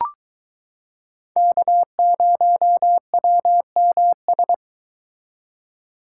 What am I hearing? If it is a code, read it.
Morse code, heard as a keyed tone, about 22 wpm, 700 Hz. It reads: K0WMS